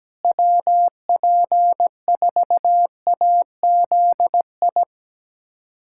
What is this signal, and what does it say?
Morse code, heard as a keyed tone, about 17 wpm, 700 Hz. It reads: WP4AZI
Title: USA Callsign DE Callsign 17wpm